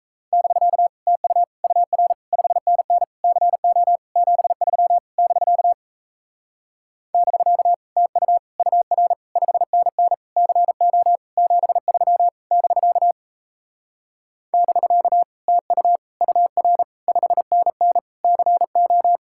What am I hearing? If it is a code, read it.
Morse code, heard as a keyed tone, about 38 wpm, 700 Hz. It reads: <BK> TU UR 5NN CO 73 <BK> <BK> TU UR 5NN CO 73 <BK> <BK> TU UR 5NN CO